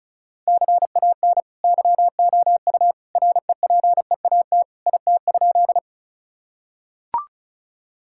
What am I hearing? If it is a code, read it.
Morse code, heard as a keyed tone, about 34 wpm, 700 Hz. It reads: CAN YOU REPEAT IT?
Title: Sentences from Top 1600 Words 35wpm